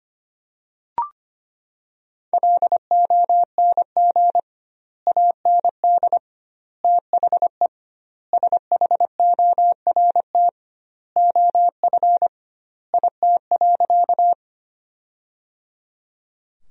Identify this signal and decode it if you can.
Morse code, heard as a keyed tone, about 25 wpm, 700 Hz. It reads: LONG AND THE SHORT OF IT.
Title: 2x Word-Spacing - Sentences from Top 400 Words - 25wpm